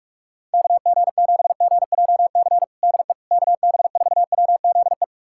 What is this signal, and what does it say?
Morse code, heard as a keyed tone, about 43 wpm, 700 Hz. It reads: KG7GJC DE KB4WZE
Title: USA Callsign DE Callsign 45wpm